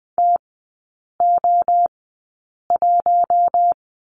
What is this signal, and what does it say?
Morse code, heard as a keyed tone, about 20 wpm, spaced as for 8 wpm, 700 Hz. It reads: TO1
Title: 041 Lesson 8e - Number 1 - Three Characters 20wpm